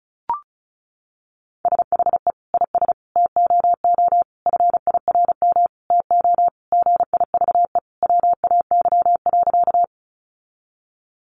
Morse code, heard as a keyed tone, about 34 wpm, 700 Hz. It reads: SHE IS TOO FIRM TO GIVE WAY.
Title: Sentences from Top 1400 Words 35wpm